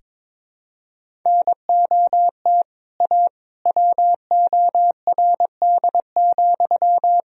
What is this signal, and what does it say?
Morse code, heard as a keyed tone, about 22 wpm, 700 Hz. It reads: NOT A WORD,
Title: Sentences from Top 1000 Words - Review Part 1 - 22wpm